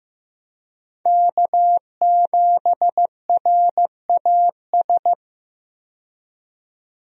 Morse code, heard as a keyed tone, about 15 wpm, 700 Hz. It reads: K7RAS
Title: USA Call Signs - 5 Characters - 15wpm